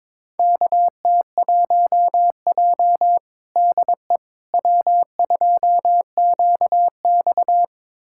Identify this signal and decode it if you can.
Morse code, heard as a keyed tone, about 22 wpm, 700 Hz. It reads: KT1J DE W2QX
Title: USA Callsign DE Callsign - 4 Characters - 22wpm